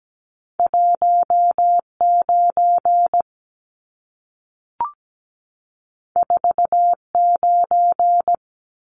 Morse code, heard as a keyed tone, about 17 wpm, 700 Hz. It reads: 19 49
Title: Numbers - 2-Digit 17wpm